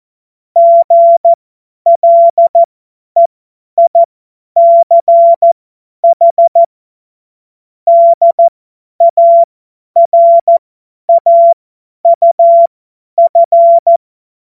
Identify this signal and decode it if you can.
Morse code, heard as a keyed tone, about 14 wpm, spaced as for 10 wpm, 680 Hz. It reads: GLEICH DARAUF
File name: Morsekurs / Witze / Witz 16. Februar 2021, 50 F70